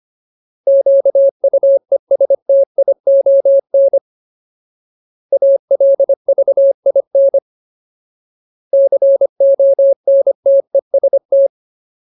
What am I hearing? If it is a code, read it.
Morse code, heard as a keyed tone, about 25 wpm, 550 Hz. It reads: QUESTION ALVIN CONTEST
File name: Ultimate Word List / 25WPM Ultimate